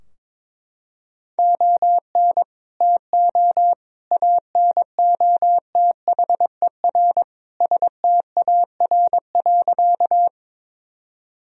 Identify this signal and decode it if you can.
Morse code, heard as a keyed tone, about 21 wpm, 700 Hz. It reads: ON TO ANOTHER STAR.
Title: Sentences from Top 1000 Words - Review Part 2 - 22wpm